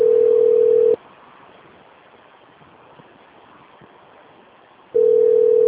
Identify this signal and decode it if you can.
Ringback tone (North America), 440 + 480 Hz.
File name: rg-1600-unknown- - - .348